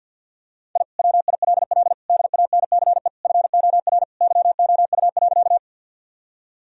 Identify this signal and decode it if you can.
Morse code, heard as a keyed tone, about 50 wpm, 700 Hz. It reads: I WILL DANCE FOR YOU.
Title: Sentences from Top 1000 Words - Review Part 4 - 50wpm